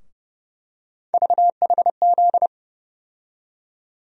Morse code, heard as a keyed tone, about 30 wpm, 700 Hz. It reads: VHZ
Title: USA Call Sign Postfixes - 3 Letters 30wpm